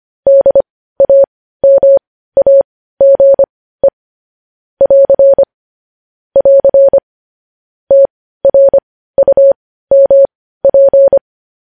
Morse code, heard as a keyed tone, about 25 wpm, spaced as for 15 wpm, 550 Hz. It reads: DAMAGE <AR> <AR> TRUMP
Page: news-headlines-in-morse-code-at-15-wpm